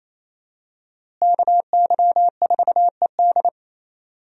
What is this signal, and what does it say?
Morse code, heard as a keyed tone, about 28 wpm, 700 Hz. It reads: KY4ED
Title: USA Call Signs - 5 Characters - Repeated 2x - 28wpm